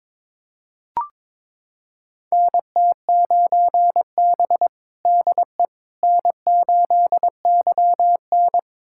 Morse code, heard as a keyed tone, about 22 wpm, 700 Hz. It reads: NT9B DE N8YN
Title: USA Callsign DE Callsign - 4 Characters - 22wpm